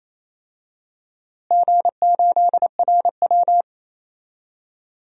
Morse code, heard as a keyed tone, about 27 wpm, 700 Hz. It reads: G8RW
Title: DXCC-100 Call Signs - 4 Characters - Repeated 2x - 28wpm